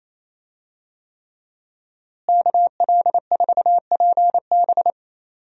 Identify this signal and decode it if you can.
Morse code, heard as a keyed tone, about 27 wpm, 700 Hz. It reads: KL4PB
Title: USA Call Signs - 5 Characters - Repeated 3x - 28wpm